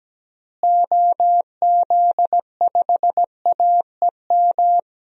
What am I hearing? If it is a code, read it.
Morse code, heard as a keyed tone, about 17 wpm, 700 Hz. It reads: OZ5AEM
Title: DXCC-100 Call Signs - 6 Characters - Repeated 3x - 17wpm